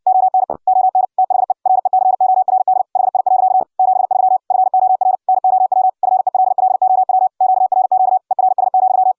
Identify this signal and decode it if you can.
Morse code, heard as a keyed tone, about 60 wpm, 750 Hz. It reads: ON ON THE RECORD HI. 73 FOR NOW FELLOW QRQ ERS,